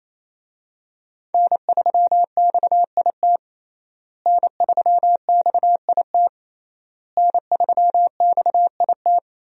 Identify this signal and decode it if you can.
Morse code, heard as a keyed tone, about 28 wpm, 700 Hz. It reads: N3XIT N3XIT N3XIT
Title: USA Call Signs - 5 Characters - Repeated 3x - 28wpm